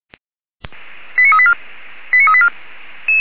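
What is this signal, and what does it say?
Five-tone selective call (ZVEI), heard as quick runs of five tones, tones of about 70 ms. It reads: 89384 89384
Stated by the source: ZVEI five-tone selective call 89145 received on SDR.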